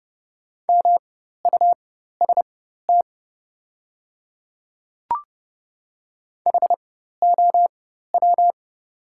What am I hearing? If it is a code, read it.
Morse code, heard as a keyed tone, about 30 wpm, spaced as for 14 wpm, 700 Hz. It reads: MUST HOW
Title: 4x Character Spacing - Top 200 Words - 30wpm